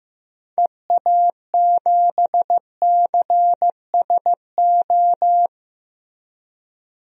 Morse code, heard as a keyed tone, about 15 wpm, 700 Hz. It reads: EA7CSO
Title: DXCC-100 Call Signs - 6 Characters - Repeated 3x - 15wpm